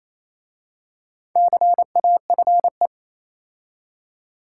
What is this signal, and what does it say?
Morse code, heard as a keyed tone, about 27 wpm, 690 Hz. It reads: CAFE
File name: Easy Words - Speed-Racing with Hints - 22wpm